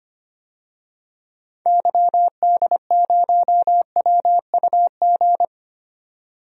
Morse code, heard as a keyed tone, about 25 wpm, 700 Hz. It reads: YD0WUG
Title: DXCC-100 Call Signs - 6 Characters - 25wpm